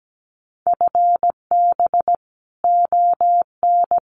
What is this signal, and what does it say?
Morse code, heard as a keyed tone, about 17 wpm, 700 Hz. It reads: FB ON
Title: Common QSO Phrases 17wpm